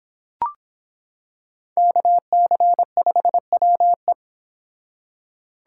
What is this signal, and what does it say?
Morse code, heard as a keyed tone, about 25 wpm, 700 Hz. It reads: KC5WE